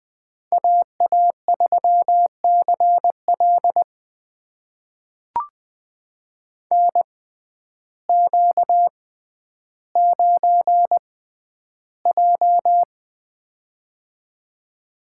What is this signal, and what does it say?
Morse code, heard as a keyed tone, about 20 wpm, spaced as for 11 wpm, 700 Hz. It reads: AA3CL NQ9J
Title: USA Call Signs - 6x Character Spacing - 20wpm